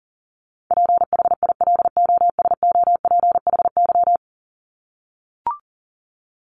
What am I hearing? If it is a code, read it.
Morse code, heard as a keyed tone, about 40 wpm, 700 Hz. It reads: PHILOSOPHY
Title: Top 751-1000 Words 40wpm